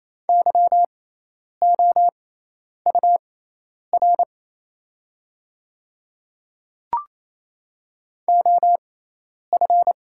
Morse code, heard as a keyed tone, about 28 wpm, spaced as for 10 wpm, 700 Hz. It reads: YOUR OF
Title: 6x Character Spacing - Top 100 Words - 28wpm